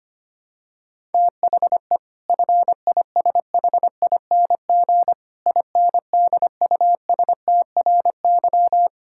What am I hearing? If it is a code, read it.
Morse code, heard as a keyed tone, about 25 wpm, 700 Hz. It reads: THE FISHING INDUSTRY
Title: Sentences from Top 1000 Words - Review Part 6 - 25wpm